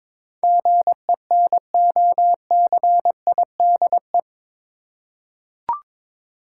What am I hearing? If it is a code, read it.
Morse code, heard as a keyed tone, about 22 wpm, 700 Hz. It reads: GENOCIDE
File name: practice / 22wpm / Eight Letter Words - 22wpm